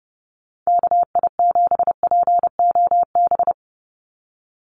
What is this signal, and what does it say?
Morse code, heard as a keyed tone, about 30 wpm, 700 Hz. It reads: KI7POB
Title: USA Call Signs 30wpm